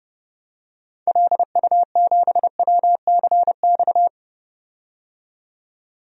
Morse code, heard as a keyed tone, about 30 wpm, 700 Hz. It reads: LU7WCX
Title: DXCC-100 Call Signs - 6 Characters - 30wpm